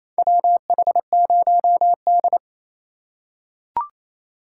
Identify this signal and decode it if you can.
Morse code, heard as a keyed tone, about 28 wpm, 700 Hz. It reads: WH0D
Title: USA Call Signs - 4x Character Spacing - 28wpm